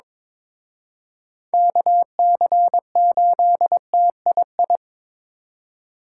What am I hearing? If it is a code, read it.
Morse code, heard as a keyed tone, about 26 wpm, 700 Hz. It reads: KC8TII
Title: Speed Racing - USA 2x3 Call Signs 22wpm